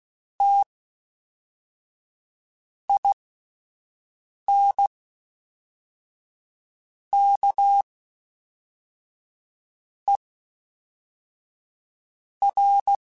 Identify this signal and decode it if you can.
Morse code, heard as a keyed tone, about 16 wpm, spaced as for 4 wpm, 800 Hz. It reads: TINKER
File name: Special INGs / Special INGs-mp3s / Fast ing @10